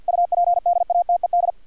Morse code, heard as a keyed tone, about 50 wpm, 700 Hz, with radio noise. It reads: UPDATED
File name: archived / CRSnet